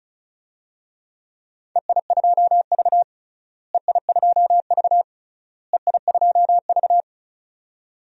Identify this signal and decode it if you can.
Morse code, heard as a keyed tone, about 36 wpm, 700 Hz. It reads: EI2V EI2V EI2V
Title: DXCC-100 Call Signs - 4 Characters - Repeated 3x - 35wpm